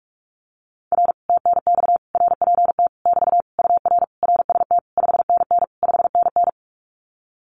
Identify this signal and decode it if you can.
Morse code, heard as a keyed tone, about 44 wpm, 700 Hz. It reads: R TNX RPT <BT> UR RST 5NN 5NN
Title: Common QSO Phrases Encore 45wpm